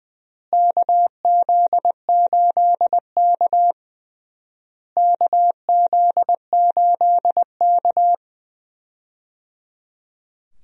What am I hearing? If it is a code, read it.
Morse code, heard as a keyed tone, about 20 wpm, 700 Hz. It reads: KZ8K KZ8K
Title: USA Call Signs - 4 Characters - Repeated 3x - 20wpm